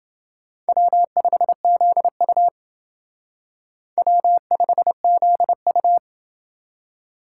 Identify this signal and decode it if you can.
Morse code, heard as a keyed tone, about 28 wpm, 700 Hz. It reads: W5ZU W5ZU